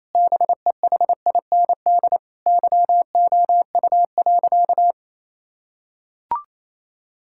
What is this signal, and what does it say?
Morse code, heard as a keyed tone, about 28 wpm, 700 Hz. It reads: BEHIND YOU.
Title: Sentences from Top 1000 Words - Review Part 6 - 28wpm